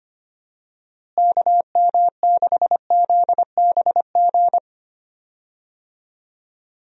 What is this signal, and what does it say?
Morse code, heard as a keyed tone, about 25 wpm, 700 Hz. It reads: KM6ZBG
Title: USA Call Signs - 6 Characters - Repeated 2x - 25wpm